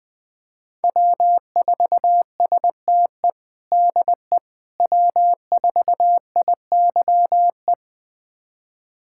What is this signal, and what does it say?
Morse code, heard as a keyed tone, about 21 wpm, 700 Hz. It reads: W4STE DE W4IYE
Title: USA Callsign DE Callsign - 5 Characters - 20wpm